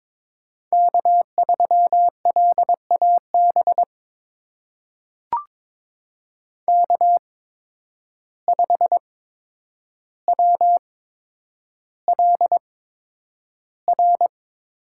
Morse code, heard as a keyed tone, about 22 wpm, spaced as for 9 wpm, 700 Hz. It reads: K3LAB K5WLR
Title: USA Call Signs - 5 Characters - 8x Character Spacing - 22wpm